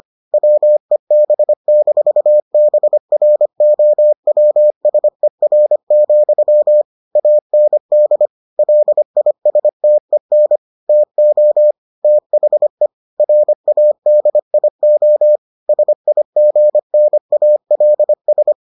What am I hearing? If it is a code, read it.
Morse code, heard as a keyed tone, about 25 wpm, 600 Hz. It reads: WEB-BROWSER, AND LISTEN TO THE RADIO SIGNALS